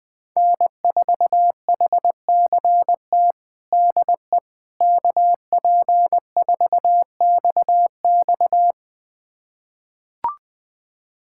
Morse code, heard as a keyed tone, about 21 wpm, 700 Hz. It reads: N4HCT DE KP4XX
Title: USA Callsign DE Callsign - 5 Characters - 20wpm